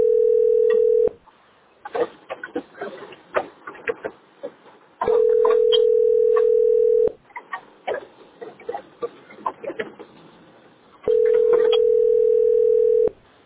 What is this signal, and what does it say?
Ringback tone (North America), 440 + 480 Hz.